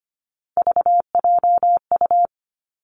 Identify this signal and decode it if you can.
Morse code, heard as a keyed tone, about 25 wpm, 700 Hz. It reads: VJU
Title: Three Challenging Letter-Number Combinations 25wpm